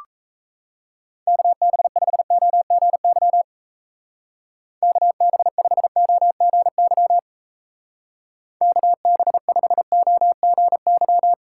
Morse code, heard as a keyed tone, about 35 wpm, 700 Hz. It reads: KB5OGY KB5OGY KB5OGY